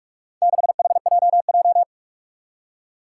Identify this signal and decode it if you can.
Morse code, heard as a keyed tone, about 46 wpm, 700 Hz. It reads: 6HJJ
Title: Four Challenging Letter-Number Combinations 45wpm